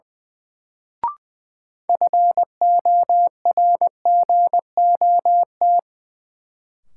Morse code, heard as a keyed tone, about 21 wpm, 700 Hz. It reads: FORGOT
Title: Sentences from Top 1400 Words - New Word Drill Alt - 20wpm